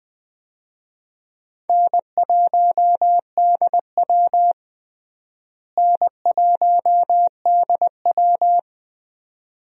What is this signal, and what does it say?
Morse code, heard as a keyed tone, about 20 wpm, 700 Hz. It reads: N1DW N1DW
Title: USA Call Signs - 4 Characters - Repeated 2x - 20wpm